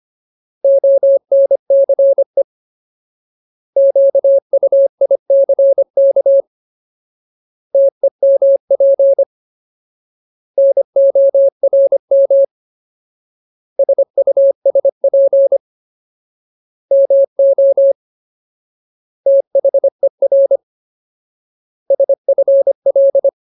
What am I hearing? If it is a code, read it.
Morse code, heard as a keyed tone, about 25 wpm, spaced as for 22 wpm, 550 Hz. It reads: ONCE QUICK TEMP NORM SUSP MOTHER SFL
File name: Ultimate Word List / 25WPM Ultimate